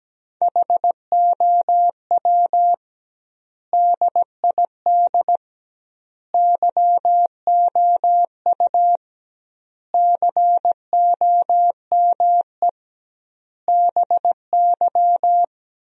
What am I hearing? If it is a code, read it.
Morse code, heard as a keyed tone, about 17 wpm, 700 Hz. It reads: HOW DID YOU COME BY
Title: 2x Word-Spacing - Sentences from Top 100 Words - 17wpm